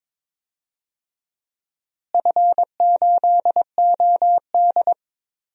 Morse code, heard as a keyed tone, about 22 wpm, 700 Hz. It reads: F8OD
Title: DXCC-100 Call Signs - 4 Characters - Repeated 2x - 22wpm